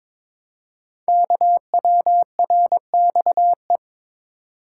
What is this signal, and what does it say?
Morse code, heard as a keyed tone, about 22 wpm, 700 Hz. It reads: KWRXE
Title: Code Groups - 5 Letters - 22wpm